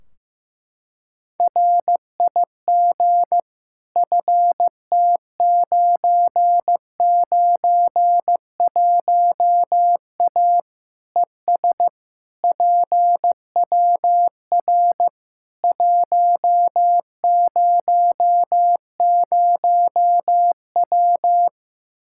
Morse code, heard as a keyed tone, about 15 wpm, 700 Hz. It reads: RIG FT991A ES PWR 100W
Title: Practice QSO - Exchange 2 - Info - 15wpm